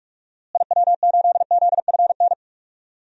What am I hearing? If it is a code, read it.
Morse code, heard as a keyed tone, about 46 wpm, 700 Hz. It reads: IW8ZFN
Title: DXCC-100 Call Signs - 6 Characters - Repeated 3x - 45wpm